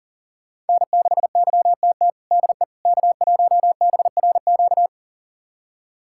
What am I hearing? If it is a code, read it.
Morse code, heard as a keyed tone, about 40 wpm, 700 Hz. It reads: N6YTT DE K1BRQ
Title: USA Callsign DE Callsign 40wpm